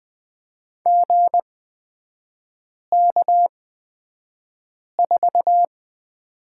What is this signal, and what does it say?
Morse code, heard as a keyed tone, about 20 wpm, spaced as for 5 wpm, 700 Hz. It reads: GK4
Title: 250 Lesson 40e - BK - Three Characters 20wpm